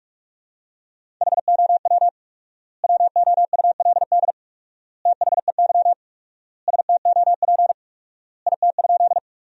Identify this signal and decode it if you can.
Morse code, heard as a keyed tone, about 46 wpm, 700 Hz. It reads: HOW WOULD THEY STOP IT?
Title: 4x Word-Spacing - Sentences from Top 300 Words - 45wpm